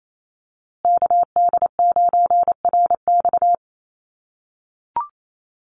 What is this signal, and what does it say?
Morse code, heard as a keyed tone, about 28 wpm, 700 Hz. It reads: KD9RX
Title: USA Call Signs Repeated 3x 28wpm